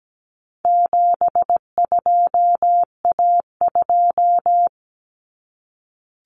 Morse code, heard as a keyed tone, about 17 wpm, 700 Hz. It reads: 72A2
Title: Common Cut Numbers - 4-Digit 17wpm